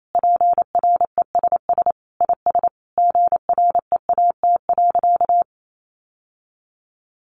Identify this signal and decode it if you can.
Morse code, heard as a keyed tone, about 28 wpm, 700 Hz. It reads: PRESS IS GREAT.
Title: Sentences from Top 800 Words 28wpm